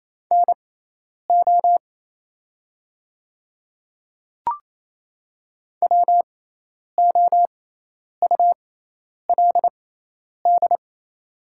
Morse code, heard as a keyed tone, about 28 wpm, spaced as for 10 wpm, 700 Hz. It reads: NO WOULD
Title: 6x Character Spacing - Top 300 Words - 28wpm